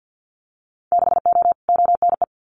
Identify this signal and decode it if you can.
Morse code, heard as a keyed tone, about 50 wpm, 700 Hz. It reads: 6O ONE
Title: Winter Field Day Exchanges Encore 50wpm